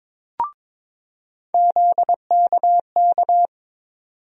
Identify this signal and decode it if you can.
Morse code, heard as a keyed tone, about 21 wpm, 700 Hz. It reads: ZKK
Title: USA Call Sign Postfixes - 3 Letters 22wpm